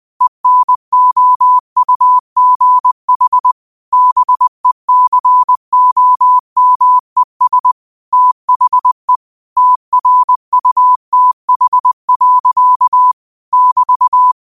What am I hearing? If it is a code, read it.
Morse code, heard as a keyed tone, about 20 wpm, 1000 Hz. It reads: ENOUGH BECOMES THE TRUTH. <BT>